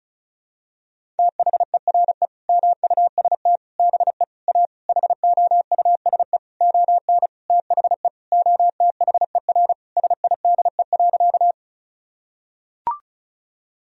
Morse code, heard as a keyed tone, about 34 wpm, 700 Hz. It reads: THERE MUST BE A HOUSE ON THE OTHER SIDE.
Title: Sentences from Top 1000 Words - Review Part 1 - 35wpm